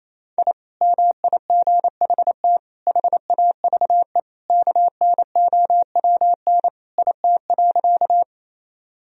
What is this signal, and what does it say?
Morse code, heard as a keyed tone, about 28 wpm, 700 Hz. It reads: I MIGHT HAVE KNOWN IT.
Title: Sentences from Top 1000 Words - Review Part 3 - 28wpm